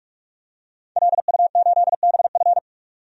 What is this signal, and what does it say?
Morse code, heard as a keyed tone, about 42 wpm, 700 Hz. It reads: LU8BF
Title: DXCC-100 Call Signs - 5 Characters - 45wpm